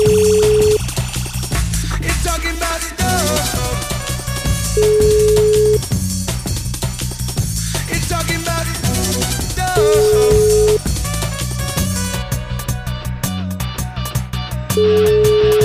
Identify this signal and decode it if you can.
Ringback tone (Europe, 425 Hz), with noise.